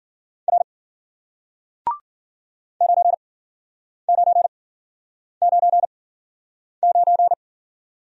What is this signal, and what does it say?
Morse code, heard as a keyed tone, about 52 wpm, spaced as for 9 wpm, 700 Hz. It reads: R 9999